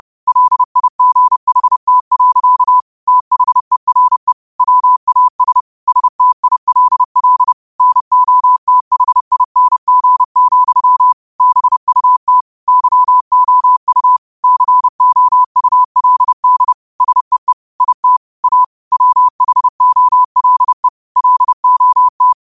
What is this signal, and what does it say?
Morse code, heard as a keyed tone, about 30 wpm, 1000 Hz. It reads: LIGHT. THERE WAS STILL NOTHING, BUT YOU COULD SEE IT A WHOLE LOT